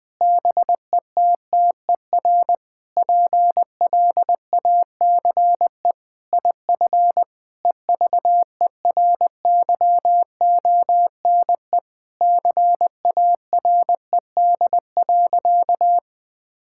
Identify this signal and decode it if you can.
Morse code, heard as a keyed tone, about 21 wpm, 700 Hz. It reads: BETTER PLACE IF EVERYONE CARED.
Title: Sentences with 8-Word Phrases - Spoken Beginning - 20wpm